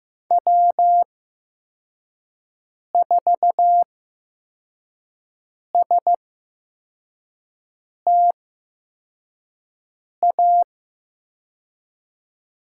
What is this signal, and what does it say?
Morse code, heard as a keyed tone, about 15 wpm, spaced as for 4 wpm, 700 Hz. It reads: W4STA
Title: USA Call Signs - 5 Characters - 8x Character Spacing - 15wpm